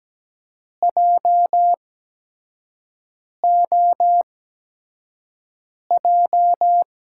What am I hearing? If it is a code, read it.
Morse code, heard as a keyed tone, about 17 wpm, spaced as for 5 wpm, 700 Hz. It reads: JOJ
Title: USA Call Signs - 6 Characters - 8x Character Spacing - 17wpm